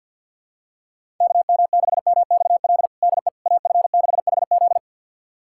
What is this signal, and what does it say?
Morse code, heard as a keyed tone, about 50 wpm, 700 Hz. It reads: KM6MXL DE AF6HZ
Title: USA Callsign DE Callsign 50wpm